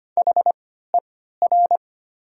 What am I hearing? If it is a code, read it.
Morse code, heard as a keyed tone, about 25 wpm, spaced as for 14 wpm, 700 Hz. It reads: HER
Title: 3x Character Spacing - Top 500 Words - 25wpm